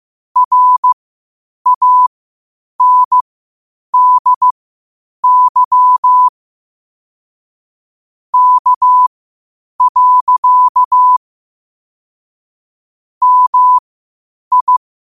Morse code, heard as a keyed tone, about 15 wpm, spaced as for 8 wpm, 1000 Hz. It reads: RANDY K. MI
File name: qotdfiles / 8wpm